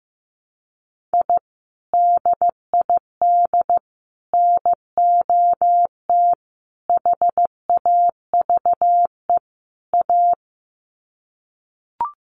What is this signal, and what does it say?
Morse code, heard as a keyed tone, about 15 wpm, 700 Hz. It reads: I DID NOT HAVE A
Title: Sets of 5 Words 15wpm